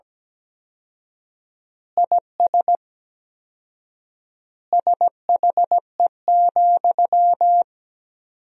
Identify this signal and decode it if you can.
Morse code, heard as a keyed tone, about 17 wpm, 700 Hz. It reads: IS SHE,
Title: 4x Word-Spacing - Sentences from Top 200 Words - 17wpm